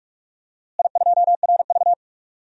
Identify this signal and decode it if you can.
Morse code, heard as a keyed tone, about 41 wpm, 700 Hz. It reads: I2RV
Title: DXCC-100 Call Signs - 4 Characters - Repeated 2x - 45wpm